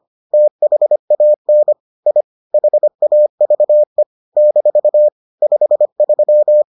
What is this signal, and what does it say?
Morse code, heard as a keyed tone, about 25 wpm, 600 Hz. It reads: THAN I HAVE - 53